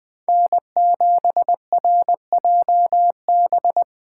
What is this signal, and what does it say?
Morse code, heard as a keyed tone, about 20 wpm, 700 Hz. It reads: N7RJB